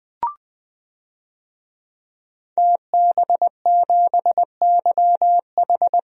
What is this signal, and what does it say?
Morse code, heard as a keyed tone, about 21 wpm, 700 Hz. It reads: TB7YH
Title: DXCC-100 Call Signs - 5 Characters - Repeated 3x - 20wpm